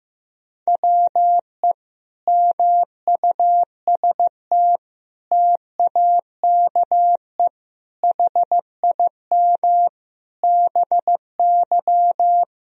Morse code, heard as a keyed tone, about 14 wpm, 700 Hz. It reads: WE MUST TAKE HIM BY
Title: Sentences from Top 1000 Words - Review Part 1 - 15wpm